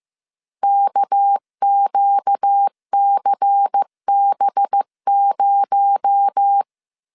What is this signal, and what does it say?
Morse code, heard as a keyed tone, about 15 wpm, 800 Hz. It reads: KQCB0
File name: CW / DET / 1 6 1500C S B 14wpm 2958